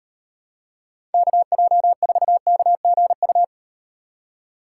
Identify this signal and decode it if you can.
Morse code, heard as a keyed tone, about 38 wpm, 700 Hz. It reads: KJ4KGU